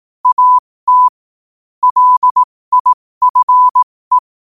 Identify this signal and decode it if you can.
Morse code, heard as a keyed tone, about 18 wpm, spaced as for 15 wpm, 1000 Hz. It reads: AT LIFE
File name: 15wpm